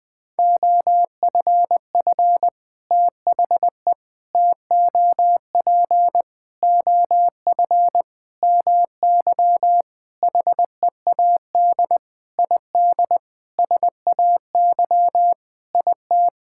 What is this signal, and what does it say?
Morse code, heard as a keyed tone, about 20 wpm, 700 Hz. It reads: OFF THE TOP OF MY HEAD ID SAY IT
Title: Sentences with 6-Word Phrases - Spoken Beginning - 20wpm